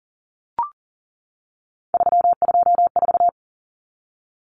Morse code, heard as a keyed tone, about 40 wpm, 700 Hz. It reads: 324
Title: Common Cut Numbers - 3-Digit 40wpm